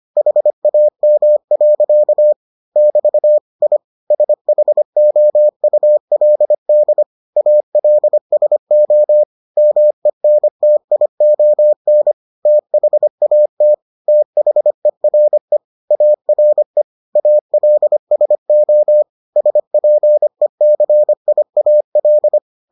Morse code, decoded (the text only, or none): HAM. <BT> I SHOULD ALSO MENTION THAT THERE ARE ALSO SPECIAL